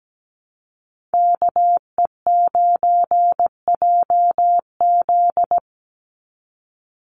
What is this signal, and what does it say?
Morse code, heard as a keyed tone, about 17 wpm, 700 Hz. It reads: KE9JZ
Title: USA Call Signs 17wpm